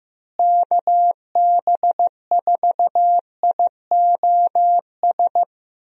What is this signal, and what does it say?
Morse code, heard as a keyed tone, about 16 wpm, 700 Hz. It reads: KB4IOS
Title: USA Call Signs - 6 Characters - Repeated 2x - 15wpm